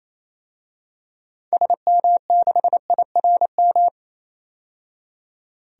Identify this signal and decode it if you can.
Morse code, heard as a keyed tone, about 27 wpm, 700 Hz. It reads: SM6IRM
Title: DXCC-100 Call Signs - 6 Characters - 28wpm